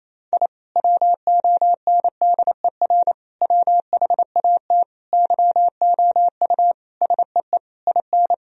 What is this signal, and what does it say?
Morse code, heard as a keyed tone, about 27 wpm, 700 Hz. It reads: I WONDER WHAT YOU SEE IN
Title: Sentences from Top 1000 Words - Review Part 3 - 28wpm